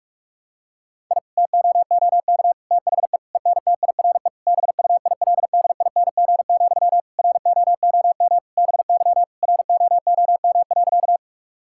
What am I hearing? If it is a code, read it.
Morse code, heard as a keyed tone, about 46 wpm, 700 Hz. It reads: I TOOK THE ENTIRE BUILDING, ROOM BY ROOM.
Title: Sentences from Top 1000 Words - Review Part 5 - 45wpm